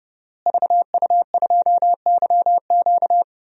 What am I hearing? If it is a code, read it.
Morse code, heard as a keyed tone, about 30 wpm, 700 Hz. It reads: VU2YQ
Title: DXCC-100 Call Signs - 5 Characters - Repeated 3x - 30wpm